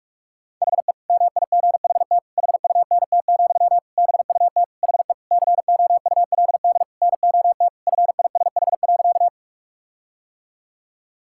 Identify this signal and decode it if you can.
Morse code, heard as a keyed tone, about 43 wpm, 700 Hz. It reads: HE MIGHT HUNT, BUT HE COULD NOT FISH.
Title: Sentences from Top 1600 Words 45wpm